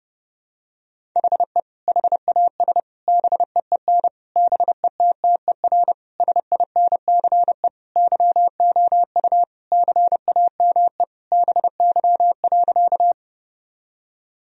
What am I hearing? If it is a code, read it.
Morse code, heard as a keyed tone, about 30 wpm, 700 Hz. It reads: HE HAS BEEN BETTER SINCE YOU CAME BY.
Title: Sentences from Top 1000 Words - Review Part 1 - 30wpm